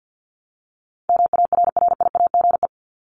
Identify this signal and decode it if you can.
Morse code, heard as a keyed tone, about 50 wpm, 700 Hz. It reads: MARRIAGE
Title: Top 751-1000 Words 50wpm